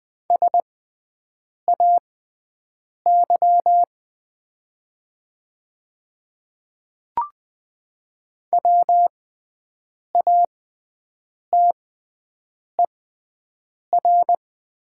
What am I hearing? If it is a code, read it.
Morse code, heard as a keyed tone, about 20 wpm, spaced as for 7 wpm, 700 Hz. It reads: SAY WATER
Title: 6x Character Spacing - Top 200 Words - 20wpm